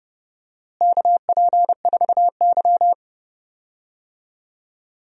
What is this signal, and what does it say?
Morse code, heard as a keyed tone, about 30 wpm, 700 Hz. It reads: KP4Y
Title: DXCC-100 Call Signs - 4 Characters - Repeated 2x - 30wpm